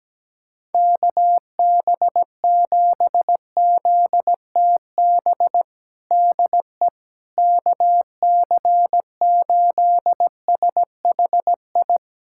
Morse code, heard as a keyed tone, about 17 wpm, 700 Hz. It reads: KB7ZTB DE KC8SHI
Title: USA Callsign DE Callsign - 6 Characters - 17wpm